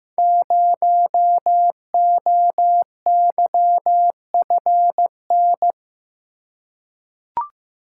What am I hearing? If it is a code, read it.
Morse code, heard as a keyed tone, about 15 wpm, 700 Hz. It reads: 0OYFN
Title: Code Groups - 5 Letter:Numbers - 15wpm